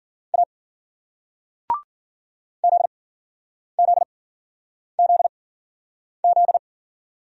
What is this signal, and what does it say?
Morse code, heard as a keyed tone, about 50 wpm, spaced as for 9 wpm, 700 Hz. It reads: A ZZZZ
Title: Speed-Racing Single Letter-Number 40wpm